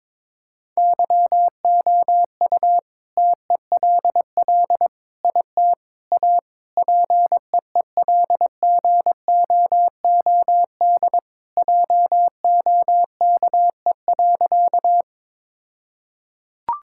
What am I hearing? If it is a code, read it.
Morse code, heard as a keyed tone, about 22 wpm, 690 Hz. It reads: YOU TELL IT A PEELGOOD JOKE.
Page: morse-code-ninja-practice-22wpm